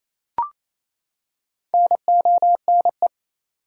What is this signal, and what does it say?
Morse code, heard as a keyed tone, about 28 wpm, 700 Hz. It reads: NONE
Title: Four Letter Words - 28wpm